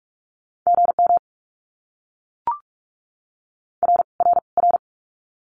Morse code, heard as a keyed tone, about 44 wpm, 700 Hz. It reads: GM R R R
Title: Abbreviations Repeated 3x - 45wpm